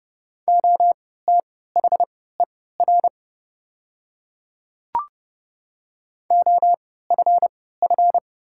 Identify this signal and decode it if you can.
Morse code, heard as a keyed tone, about 30 wpm, spaced as for 17 wpm, 700 Hz. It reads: OTHER OFF
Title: 3x Character Spacing - Top 200 Words - 30wpm